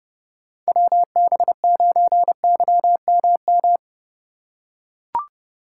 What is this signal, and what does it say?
Morse code, heard as a keyed tone, about 30 wpm, 700 Hz. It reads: WB9YMM
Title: USA Call Signs - 6x Character Spacing - 30wpm